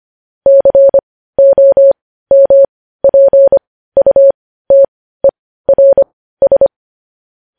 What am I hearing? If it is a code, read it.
Morse code, heard as a keyed tone, about 25 wpm, spaced as for 15 wpm, 550 Hz. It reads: COMPUTERS
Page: News Headlines in Morse Code at 15 WPM